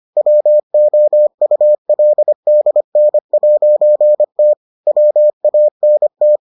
Morse code, heard as a keyed tone, about 25 wpm, 600 Hz. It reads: WOULDN'T WANT